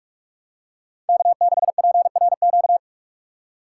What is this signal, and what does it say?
Morse code, heard as a keyed tone, about 46 wpm, 700 Hz. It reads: K6PRQ
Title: USA Call Signs - 5 Characters - 2x Character Spacing - 45wpm